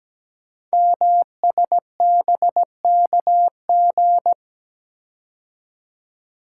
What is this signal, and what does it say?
Morse code, heard as a keyed tone, about 17 wpm, 700 Hz. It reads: MSBKG
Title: Code Groups - 5 Letters - 17wpm